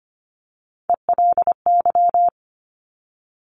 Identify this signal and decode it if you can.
Morse code, heard as a keyed tone, about 25 wpm, 700 Hz. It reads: ELY
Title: Top Three Letter Combinations in Words 25wpm